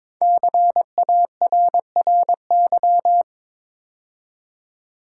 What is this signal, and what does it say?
Morse code, heard as a keyed tone, about 23 wpm, 700 Hz. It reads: CARRY
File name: 5 Letter Words - First 3 Letters Given - 22wpm